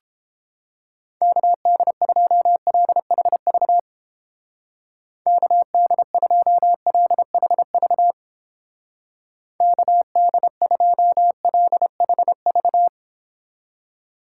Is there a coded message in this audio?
Yes: Morse code from 1.2 to 12.9 s, heard as a keyed tone, about 31 wpm, 700 Hz. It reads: KD2LHV KD2LHV KD2LHV